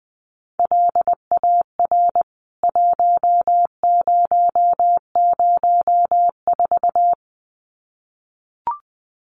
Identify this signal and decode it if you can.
Morse code, heard as a keyed tone, about 20 wpm, 700 Hz. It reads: LAR 1004
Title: CWT Exchange 20wpm